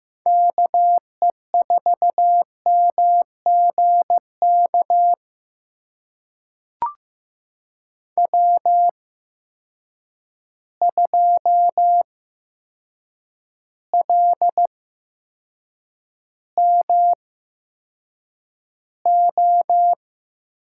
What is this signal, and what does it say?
Morse code, heard as a keyed tone, about 15 wpm, spaced as for 7 wpm, 700 Hz. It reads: KE4MGK W2LMO
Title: USA Call Signs - 8x Character Spacing - 15wpm